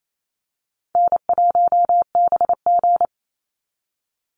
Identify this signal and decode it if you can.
Morse code, heard as a keyed tone, about 28 wpm, 700 Hz. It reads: N1BG
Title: Call Signs Repeated 2x - 28wpm